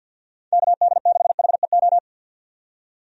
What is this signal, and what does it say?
Morse code, heard as a keyed tone, about 52 wpm, 700 Hz. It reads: KD6HEO
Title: USA Call Signs - 6 Characters - 50wpm